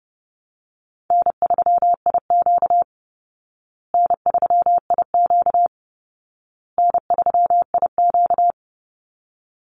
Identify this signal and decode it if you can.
Morse code, heard as a keyed tone, about 28 wpm, 700 Hz. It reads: N3IQ N3IQ N3IQ
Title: Call Signs Repeated 2x - 30wpm